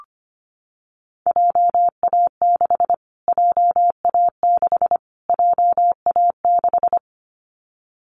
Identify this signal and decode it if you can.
Morse code, heard as a keyed tone, about 25 wpm, 700 Hz. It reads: JA6 JA6 JA6
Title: Callsign Prefixes for Top 25 DXCC - Repeated 3x - 25wpm